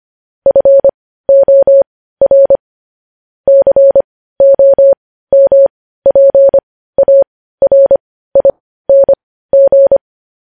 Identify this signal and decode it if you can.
Morse code, heard as a keyed tone, about 25 wpm, spaced as for 15 wpm, 550 Hz. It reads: FOR COMPARING
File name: podcast / News headlines-1x-25x15